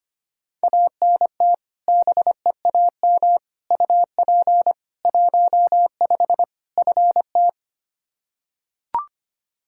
Morse code, heard as a keyed tone, about 25 wpm, 700 Hz. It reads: ANT BEAM UP 15 FT
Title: Practice QSO - Exchange 2 - Info - 25wpm